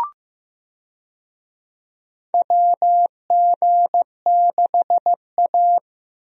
Morse code, heard as a keyed tone, about 15 wpm, 700 Hz. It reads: WG6A
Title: USA Call Signs - 4 Characters - Repeated 2x - 15wpm